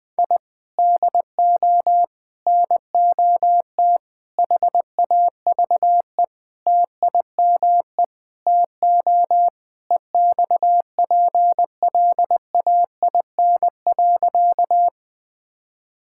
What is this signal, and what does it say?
Morse code, heard as a keyed tone, about 20 wpm, 700 Hz. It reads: I DO NOT HAVE TIME TO EXPLAIN.
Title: Sentences with 5-Word Phrases - 20wpm